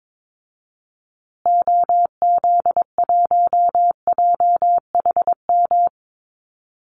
Morse code, heard as a keyed tone, about 22 wpm, 700 Hz. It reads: OZ1JHM
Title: Call Signs Repeated 2x - 22wpm